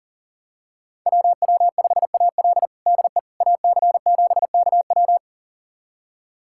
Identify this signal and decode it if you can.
Morse code, heard as a keyed tone, about 40 wpm, 700 Hz. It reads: WW5AL DE AC7KW
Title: USA Callsign DE Callsign - 5 Characters - 40wpm